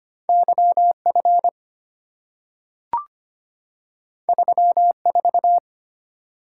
Morse code, heard as a keyed tone, about 25 wpm, 700 Hz. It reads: YF 34
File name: Two Letter-Number Combinations - 25wpm